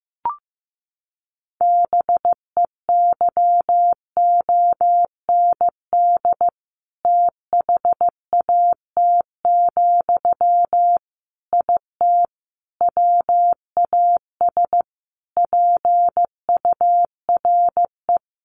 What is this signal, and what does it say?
Morse code, heard as a keyed tone, about 15 wpm, 700 Hz. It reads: BEYOND THAT, IT WAS PURE
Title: Sentences from Top 1500 Words 15wpm